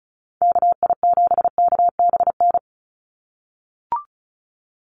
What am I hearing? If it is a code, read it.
Morse code, heard as a keyed tone, about 34 wpm, 700 Hz. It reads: KI7KBN
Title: USA Call Signs 35wpm